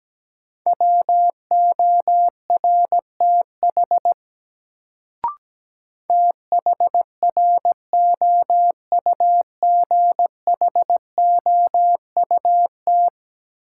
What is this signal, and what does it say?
Morse code, heard as a keyed tone, about 17 wpm, 700 Hz. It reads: WORTH THROUGHOUT
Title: Sentences from Top 800 Words - New Word Drill - 17wpm